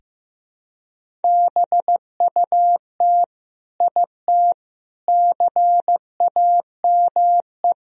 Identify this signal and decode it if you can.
Morse code, heard as a keyed tone, about 16 wpm, 700 Hz. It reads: BUT IT CAME
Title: Sentences from Top 1000 Words - Review Part 2 - 15wpm